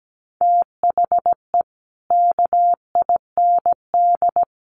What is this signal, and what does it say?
Morse code, heard as a keyed tone, about 17 wpm, 700 Hz. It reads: THE KIND
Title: Sets of 5 Words Encore 17wpm